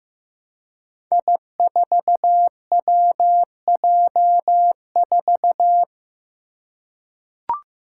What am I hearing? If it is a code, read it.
Morse code, heard as a keyed tone, about 15 wpm, 700 Hz. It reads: I4WJ4
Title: Five Challenging Letter-Number Combinations 15wpm